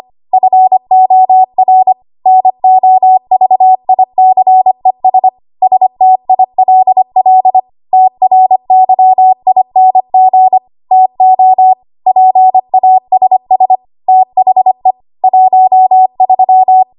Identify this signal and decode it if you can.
Morse code, heard as a keyed tone, about 25 wpm, 750 Hz. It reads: FOR NOVICES STILL TRYING TO PASS THE 13
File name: Archive / 25 WPM / 25WPM